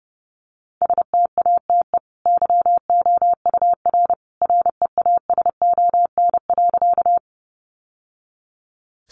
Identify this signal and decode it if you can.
Morse code, heard as a keyed tone, about 30 wpm, 700 Hz. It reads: STATE YOUR REASON.
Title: Sentences from Top 700 Words 30wpm